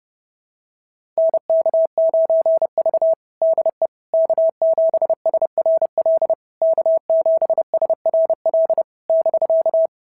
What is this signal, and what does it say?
Morse code, heard as a keyed tone, about 30 wpm, 690 Hz. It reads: NK9V DE K7SRL K7SRL <BK>
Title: SOTA QSO Practice 30wpm